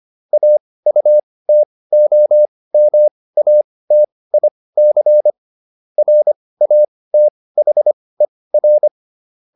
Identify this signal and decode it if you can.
Morse code, heard as a keyed tone, about 25 wpm, spaced as for 18 wpm, 600 Hz. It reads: AUTOMATIC RATHER